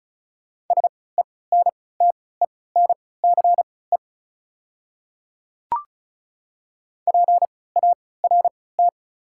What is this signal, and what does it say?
Morse code, heard as a keyed tone, about 34 wpm, spaced as for 20 wpm, 700 Hz. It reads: SENTENCE PART
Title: 3x Character Spacing - Top 200 Words - 35wpm